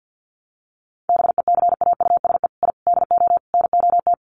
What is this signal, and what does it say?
Morse code, heard as a keyed tone, about 50 wpm, 700 Hz. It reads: BECAUSE I DO NOT
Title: Sets of 5 Words Encore2 50wpm